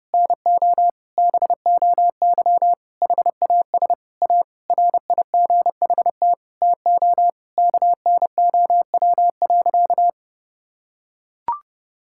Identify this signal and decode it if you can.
Morse code, heard as a keyed tone, about 30 wpm, 700 Hz. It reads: NO BOY HAS A RIGHT TO KNOW.
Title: Sentences from Top 1000 Words - Review Part 6 - 30wpm